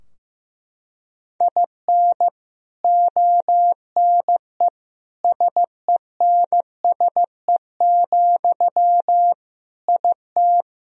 Morse code, heard as a keyed tone, about 15 wpm, 700 Hz. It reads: IN ONE SENSE, IT
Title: Sentences from Top 1000 Words - Review Part 4 - 15wpm